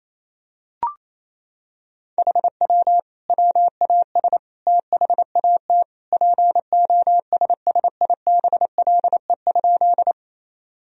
Morse code, heard as a keyed tone, about 28 wpm, 700 Hz. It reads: HW WAS THAT POSSIBLE?
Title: Sentences from Top 500 Words and Abbreviations 28wpm